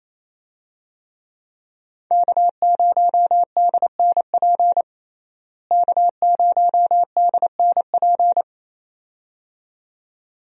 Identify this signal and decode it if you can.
Morse code, heard as a keyed tone, about 28 wpm, 700 Hz. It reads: K0DNP K0DNP
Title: USA Call Signs - 5 Characters - Repeated 2x - 28wpm